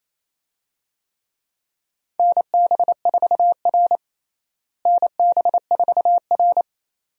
Morse code, heard as a keyed tone, about 28 wpm, 700 Hz. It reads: NB4R NB4R
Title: USA Call Signs - 4 Characters - Repeated 2x - 28wpm